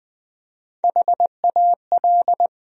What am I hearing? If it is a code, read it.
Morse code, heard as a keyed tone, about 20 wpm, 700 Hz. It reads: HAL
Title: USA Common Names - Max 5 Letters - 20wpm